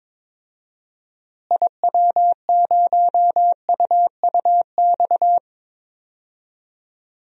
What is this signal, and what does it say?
Morse code, heard as a keyed tone, about 21 wpm, 700 Hz. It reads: IW0UUX
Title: DXCC-100 Call Signs - 6 Characters - Repeated 2x - 22wpm